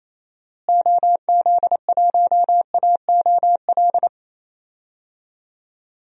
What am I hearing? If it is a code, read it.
Morse code, heard as a keyed tone, about 28 wpm, 700 Hz. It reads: OZ1AOL
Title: DXCC-100 Call Signs - 6 Characters - Repeated 2x - 28wpm